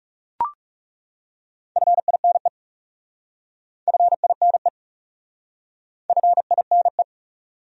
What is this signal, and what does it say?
Morse code, heard as a keyed tone, about 40 wpm, 700 Hz. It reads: FINE FINE FINE